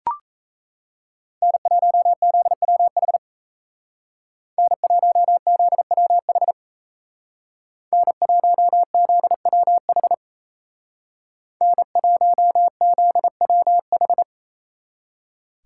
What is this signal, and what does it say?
Morse code, heard as a keyed tone, about 33 wpm, 700 Hz. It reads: N1ZWH N1ZWH N1ZWH N1ZWH
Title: Speed Racing - USA Call Signs 28wpm